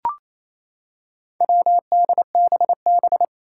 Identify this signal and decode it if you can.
Morse code, heard as a keyed tone, about 28 wpm, 700 Hz. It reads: WDBB
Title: Four Challenging Letter-Number Combinations 28wpm